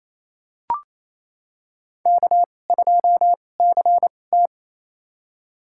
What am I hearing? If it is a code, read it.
Morse code, heard as a keyed tone, about 29 wpm, spaced as for 20 wpm, 700 Hz. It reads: K2CT
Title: USA Call Signs - 4 Characters - 2x Character Spacing - 28wpm